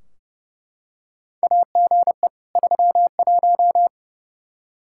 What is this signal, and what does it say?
Morse code, heard as a keyed tone, about 32 wpm, 700 Hz. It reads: AGE 31
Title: Practice QSO - Exchange 3 - More Info - 30wpm